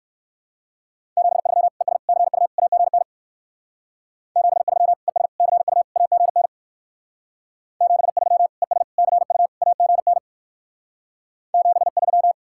Morse code, heard as a keyed tone, about 54 wpm, 700 Hz. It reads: 73 ES CU AGN 73 ES CU AGN 73 ES CU AGN 73
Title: Speed-Racing Common QSO Phrases - 45wpm